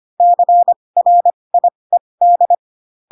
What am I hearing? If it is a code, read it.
Morse code, heard as a keyed tone, about 25 wpm, spaced as for 20 wpm, 700 Hz. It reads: CRIED
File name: Pinocchio / Pinocchio ch7 15 25 ik1hge